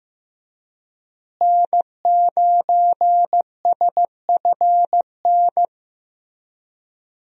Morse code, heard as a keyed tone, about 16 wpm, 700 Hz. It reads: N9SFN
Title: USA Call Signs - 5 Characters - Repeated 3x - 15wpm